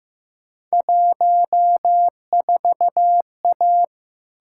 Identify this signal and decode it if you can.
Morse code, heard as a keyed tone, about 15 wpm, 700 Hz. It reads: 14A
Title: Field Day Exchanges Encore - 15wpm